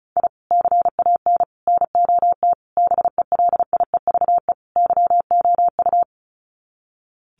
Morse code, heard as a keyed tone, about 34 wpm, 700 Hz. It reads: I CAN NOT BELIEVE YOU
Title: Sets of 5 Words Encore 35wpm